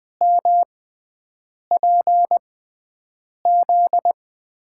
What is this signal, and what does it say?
Morse code, heard as a keyed tone, about 20 wpm, spaced as for 7 wpm, 700 Hz. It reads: MPZ
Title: USA Call Signs - 5 Characters - 6x Character Spacing - 20wpm